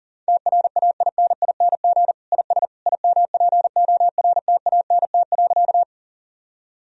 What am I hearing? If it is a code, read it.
Morse code, heard as a keyed tone, about 40 wpm, 690 Hz. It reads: TRAINING IS IMPORTANT.
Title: Sentences from Top 1000 Words - Review Part 4 - 40wpm